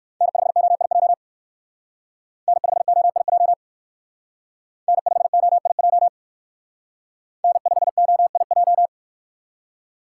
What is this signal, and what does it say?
Morse code, heard as a keyed tone, about 57 wpm, 700 Hz. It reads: N5OIJ N5OIJ N5OIJ N5OIJ